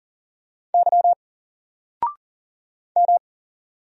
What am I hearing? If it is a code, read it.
Morse code, heard as a keyed tone, about 40 wpm, 700 Hz. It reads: Y M
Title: Single Character (Letter:Number:Punctuation:Prosign) Alt 40wpm